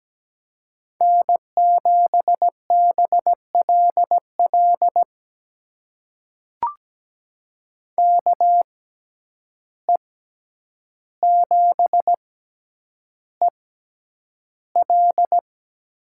Morse code, heard as a keyed tone, about 17 wpm, spaced as for 9 wpm, 700 Hz. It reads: N7BLL KE7EL
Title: USA Call Signs - 5 Characters - 6x Character Spacing - 17wpm